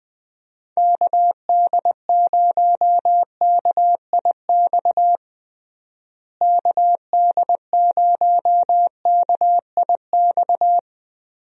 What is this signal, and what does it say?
Morse code, heard as a keyed tone, about 20 wpm, 700 Hz. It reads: KD0KIX KD0KIX
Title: USA Call Signs - 6 Characters - Repeated 3x - 20wpm